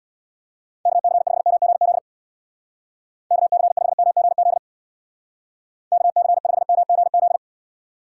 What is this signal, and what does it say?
Morse code, heard as a keyed tone, about 64 wpm, 700 Hz. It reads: KC5MGZ KC5MGZ KC5MGZ